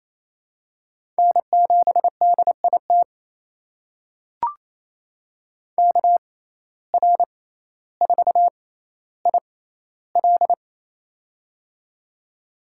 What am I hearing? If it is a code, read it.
Morse code, heard as a keyed tone, about 28 wpm, spaced as for 14 wpm, 700 Hz. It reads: N7DIT KR4IL
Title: USA Call Signs - 5 Characters - 6x Character Spacing - 28wpm